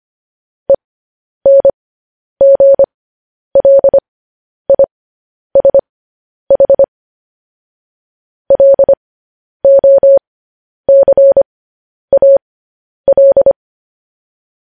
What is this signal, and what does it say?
Morse code, heard as a keyed tone, about 25 wpm, spaced as for 10 wpm, 550 Hz. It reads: ENGLISH LOCAL
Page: news-headlines-in-morse-code-at-10-wpm